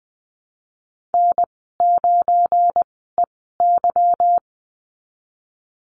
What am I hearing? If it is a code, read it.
Morse code, heard as a keyed tone, about 20 wpm, spaced as for 15 wpm, 700 Hz. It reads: N9EY
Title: 173 Lesson 28g - Number 9 - Call Signs 20wpm